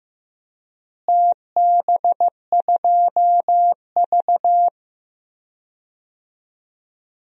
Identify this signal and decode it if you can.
Morse code, heard as a keyed tone, about 15 wpm, 700 Hz. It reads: TB2V
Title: DXCC-100 Call Signs - 4 Characters - Repeated 3x - 15wpm